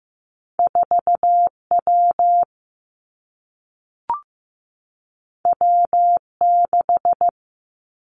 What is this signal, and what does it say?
Morse code, heard as a keyed tone, about 15 wpm, 700 Hz. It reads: 4 W W 6
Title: Two Challenging Letter-Number Combinations 15wpm